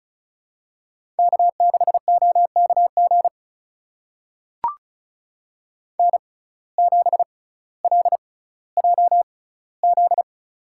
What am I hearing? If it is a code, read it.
Morse code, heard as a keyed tone, about 34 wpm, spaced as for 18 wpm, 700 Hz. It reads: K6OKG N7LJZ
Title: USA Call Signs - 5 Characters - 6x Character Spacing - 35wpm